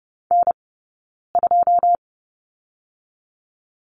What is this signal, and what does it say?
Morse code, heard as a keyed tone, about 30 wpm, spaced as for 9 wpm, 690 Hz. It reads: N2
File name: Lesson 18d - Letter m - Two Characters 30wpm